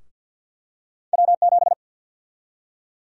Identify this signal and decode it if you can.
Morse code, heard as a keyed tone, about 53 wpm, 700 Hz. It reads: W7
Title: USA Call Sign Prefixes - 50wpm